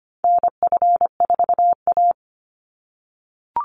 Morse code, heard as a keyed tone, about 25 wpm, 700 Hz. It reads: NF4A
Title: Call Signs - 25wpm